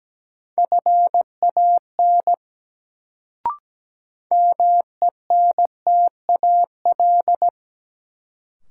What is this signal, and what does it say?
Morse code, heard as a keyed tone, about 17 wpm, 700 Hz. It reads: FAN MENTAL
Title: Sentences from Top 1000 Words - Review Part 4 - New Word Drill - 17wpm